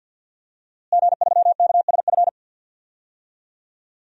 Morse code, heard as a keyed tone, about 51 wpm, 700 Hz. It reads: G3KSF
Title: DXCC-100 Call Signs - 5 Characters - Repeated 3x - 50wpm